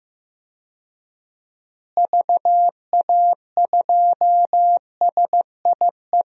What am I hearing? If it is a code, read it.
Morse code, heard as a keyed tone, about 15 wpm, 700 Hz. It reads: VA2SIE
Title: DXCC-100 Call Signs - 6 Characters - Repeated 3x - 15wpm